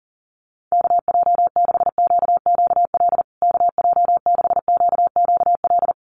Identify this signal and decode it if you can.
Morse code, heard as a keyed tone, about 40 wpm, 700 Hz. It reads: KJ6QQL KJ6QQL
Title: USA Call Signs Repeated 3x 40wpm